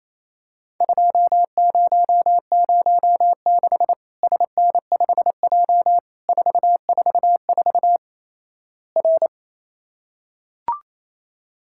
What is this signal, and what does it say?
Morse code, heard as a keyed tone, about 28 wpm, 700 Hz. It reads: 2006 SN5J 444 R
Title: Worked All Europe DX Contest - QTC Exchange - 28wpm